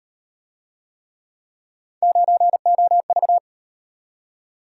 Morse code, heard as a keyed tone, about 35 wpm, 700 Hz. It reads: 9OV